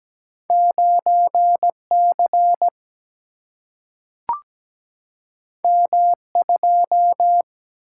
Morse code, heard as a keyed tone, about 17 wpm, 700 Hz. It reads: 9C M2
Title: Single-Letter Single-Number 17wpm